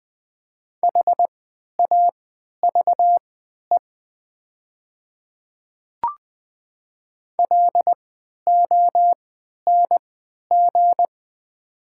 Morse code, heard as a keyed tone, about 20 wpm, spaced as for 11 wpm, 700 Hz. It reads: HAVE LONG
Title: 3x Character Spacing - Top 100 Words - 20wpm